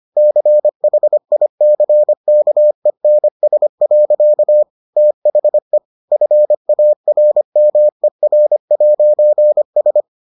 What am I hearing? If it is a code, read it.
Morse code, heard as a keyed tone, about 25 wpm, 600 Hz. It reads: CHICKENS. THE FARMER'S